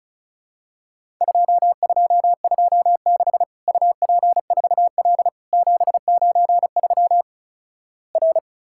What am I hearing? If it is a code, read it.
Morse code, heard as a keyed tone, about 36 wpm, 700 Hz. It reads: 2226 UP4L 793 R
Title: Worked All Europe DX Contest - QTC Exchange - 35wpm